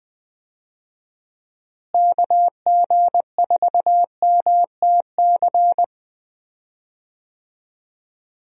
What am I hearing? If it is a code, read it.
Morse code, heard as a keyed tone, about 20 wpm, 700 Hz. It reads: KG4MTC
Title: USA Call Signs - 6 Characters - Repeated 3x - 20wpm